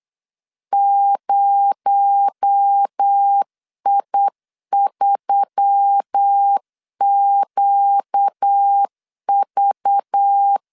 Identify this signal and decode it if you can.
Morse code, heard as a keyed tone, about 8 wpm, 800 Hz. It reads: 0I3QV
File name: LEZIONI / 6 PKX0 / 6 PKX0 8wpm 3341